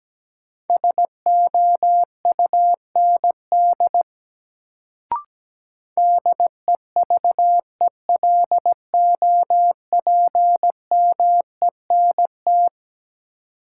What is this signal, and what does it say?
Morse code, heard as a keyed tone, about 17 wpm, 700 Hz. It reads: SOUND DEVELOPMENT
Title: Sets of 3 Words - New Word Drill - 17wpm